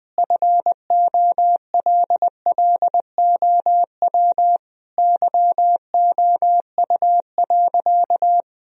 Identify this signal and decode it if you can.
Morse code, heard as a keyed tone, about 20 wpm, 700 Hz. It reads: FOLLOW YOU.
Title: Sentences from Top 1000 Words - Review Part 1 - 20wpm